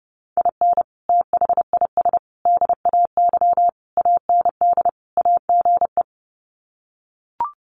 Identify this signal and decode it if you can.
Morse code, heard as a keyed tone, about 30 wpm, 700 Hz. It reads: IN THIS DAY AND AGE
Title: Sets of 5 Words Encore 30wpm